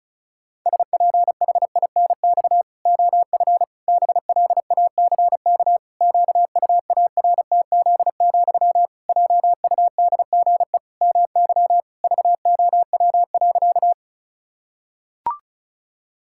Morse code, heard as a keyed tone, about 34 wpm, 700 Hz. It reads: SPHINX OF BLACK QUARTZ, JUDGE MY VOW.
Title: Pangram Challenge v2 - 35wpm